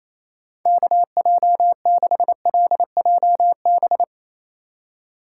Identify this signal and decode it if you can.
Morse code, heard as a keyed tone, about 28 wpm, 700 Hz. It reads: KJ6LJB
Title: USA Call Signs - 6 Characters - 3x Character Spacing - 28wpm